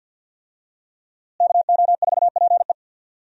Morse code, heard as a keyed tone, about 47 wpm, 700 Hz. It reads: KO4PE
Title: USA Call Signs - 5 Characters - Repeated 2x - 50wpm